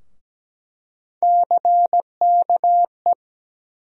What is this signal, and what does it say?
Morse code, heard as a keyed tone, about 17 wpm, 700 Hz. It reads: CKE
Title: USA Call Sign Postfixes - 3 Letters 17wpm